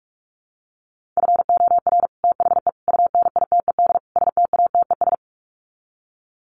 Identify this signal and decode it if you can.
Morse code, heard as a keyed tone, about 44 wpm, 700 Hz. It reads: FOR THE UNITED STATES
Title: Sets of 4 Words 45wpm